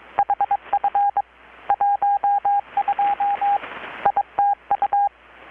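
Morse code, heard as a keyed tone, about 22 wpm, 810 Hz, with radio noise. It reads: HF 12 ITU
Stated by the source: Morse from KPH's Night of Nights XX maritime radiotelegraphy event, received in Chiapas, Mexico about 3,800 km away on a Kenwood R-600 with a grounded 10 m long-wire antenna.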